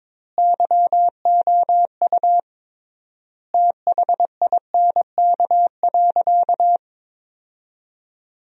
Morse code, heard as a keyed tone, about 22 wpm, 700 Hz. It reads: YOU THINK.
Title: 3x Word-Spacing - Sentences from Top 400 Words - 22wpm